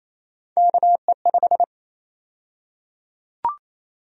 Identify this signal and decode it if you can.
Morse code, heard as a keyed tone, about 28 wpm, 700 Hz. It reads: KE5
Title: USA Call Sign Prefixes - 28wpm